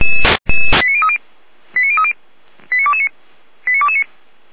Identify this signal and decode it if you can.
Five-tone selective call (ZVEI), heard as quick runs of five tones, tones of about 70 ms. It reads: 89030 89030 89229 89229